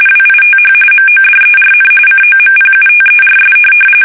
RTTY, about 45.45 baud. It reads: WITH MODERN COMPUTERS
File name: fsk850 / teletype-RTTY